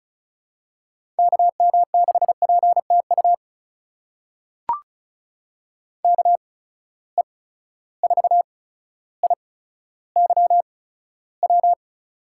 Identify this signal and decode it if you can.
Morse code, heard as a keyed tone, about 36 wpm, spaced as for 15 wpm, 700 Hz. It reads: KM6PTU KE4IYW
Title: USA Call Signs - 6 Characters - 8x Character Spacing - 35wpm